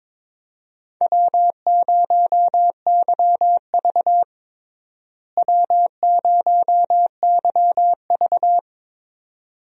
Morse code, heard as a keyed tone, about 22 wpm, 700 Hz. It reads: W0YV W0YV
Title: USA Call Signs - 4 Characters - Repeated 2x - 22wpm